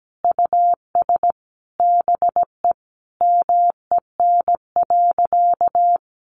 Morse code, heard as a keyed tone, about 17 wpm, 700 Hz. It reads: US BE MEN.
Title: Sentences from Top 300 Words 17wpm